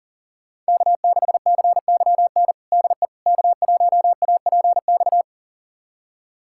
Morse code, heard as a keyed tone, about 40 wpm, 700 Hz. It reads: K6CYN DE K1APX
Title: USA Callsign DE Callsign - 5 Characters - 40wpm